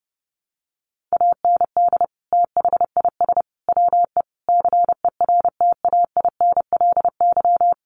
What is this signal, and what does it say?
Morse code, heard as a keyed tone, about 30 wpm, 700 Hz. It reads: AND THIS WE CERTAINLY
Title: Sentences from Top 900 Words 30wpm